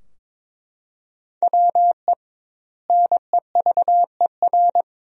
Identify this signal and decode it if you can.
Morse code, heard as a keyed tone, about 22 wpm, 700 Hz. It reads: WE NEVER
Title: 2x Word-Spacing - Sentences from Top 300 Words - 22wpm